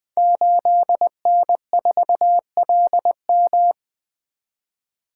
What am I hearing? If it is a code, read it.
Morse code, heard as a keyed tone, about 20 wpm, 700 Hz. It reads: 8N4LM
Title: Code Groups - 5 Letter:Numbers - 20wpm